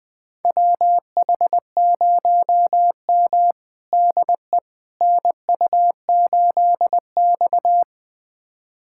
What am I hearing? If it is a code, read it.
Morse code, heard as a keyed tone, about 20 wpm, 700 Hz. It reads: WH0M DE NU8X
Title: USA Callsign DE Callsign - 4 Characters - 20wpm